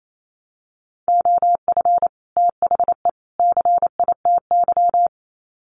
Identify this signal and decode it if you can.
Morse code, heard as a keyed tone, about 28 wpm, 700 Hz. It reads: OF THE CITY
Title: Sets of 3 Words 28wpm